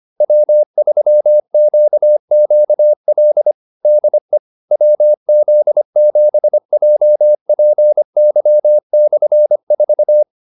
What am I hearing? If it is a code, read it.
Morse code, heard as a keyed tone, about 25 wpm, 600 Hz. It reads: W3QQL DE WZ7JPY/4
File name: QSO309a 25